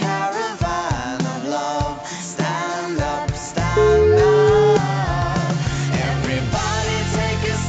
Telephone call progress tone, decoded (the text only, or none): ringback tone